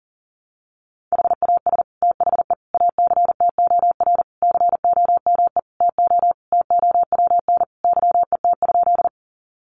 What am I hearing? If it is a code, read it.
Morse code, heard as a keyed tone, about 40 wpm, 700 Hz. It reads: HAS THE ACTOR COME TO TOWN YET?
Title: Sentences from Top 1500 Words 40wpm